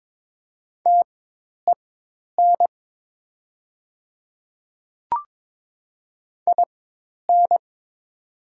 Morse code, heard as a keyed tone, about 22 wpm, spaced as for 10 wpm, 700 Hz. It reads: TEN IN
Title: 4x Character Spacing - Top 500 Words - 22wpm